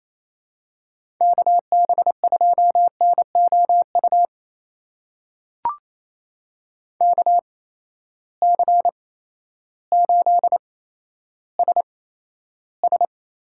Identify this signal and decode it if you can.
Morse code, heard as a keyed tone, about 28 wpm, spaced as for 13 wpm, 700 Hz. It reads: KB2NOU KC8SS
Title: USA Call Signs - 6 Characters - 8x Character Spacing - 28wpm